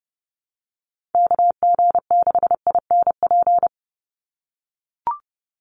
Morse code, heard as a keyed tone, about 30 wpm, 700 Hz. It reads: KG6INP
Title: USA Call Signs 30wpm